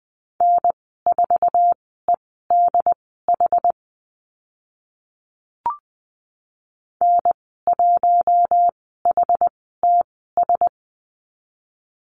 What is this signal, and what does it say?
Morse code, heard as a keyed tone, about 20 wpm, spaced as for 14 wpm, 700 Hz. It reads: N4EDH N1HTS
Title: 088 Lesson 15g - Number 5 - Call Signs 20wpm